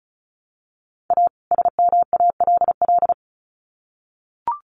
Morse code, heard as a keyed tone, about 34 wpm, 700 Hz. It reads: A SMALL
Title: Sets of 2 Words 35wpm